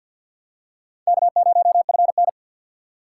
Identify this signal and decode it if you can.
Morse code, heard as a keyed tone, about 50 wpm, 700 Hz. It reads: K0FN
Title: USA Call Signs - 2x Character Spacing - 50wpm